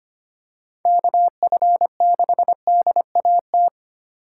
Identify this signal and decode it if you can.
Morse code, heard as a keyed tone, about 25 wpm, 700 Hz. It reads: KF6DAT
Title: USA Call Signs - 6 Characters - 4x Character Spacing - 25wpm